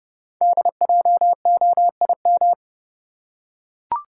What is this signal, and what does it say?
Morse code, heard as a keyed tone, about 30 wpm, 690 Hz. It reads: DJOIM
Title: Code Groups - 5 Letters - 30wpm